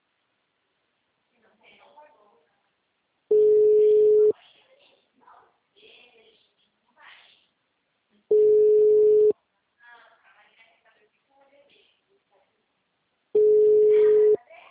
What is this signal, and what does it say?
Ringback tone (Europe, 425 Hz), with speech.